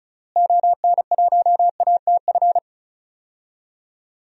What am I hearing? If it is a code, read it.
Morse code, heard as a keyed tone, about 36 wpm, 700 Hz. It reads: ON1ATF
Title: DXCC-100 Call Signs - 6 Characters - Repeated 3x - 35wpm